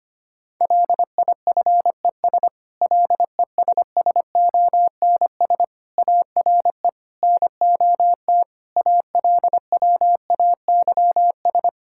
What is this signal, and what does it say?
Morse code, heard as a keyed tone, about 25 wpm, 700 Hz. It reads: LIFES LESSONS ARE NOT ALWAYS
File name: practice / 25wpm / Difficult Sentences with Hints - 25wpm